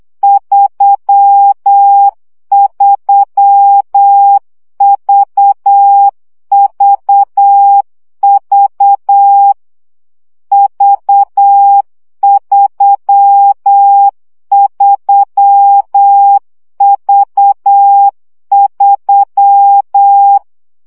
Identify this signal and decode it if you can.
Morse code, heard as a keyed tone, about 8 wpm, 800 Hz. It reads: 33VVV V33V3
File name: CW gyakorlas v 3 7 baud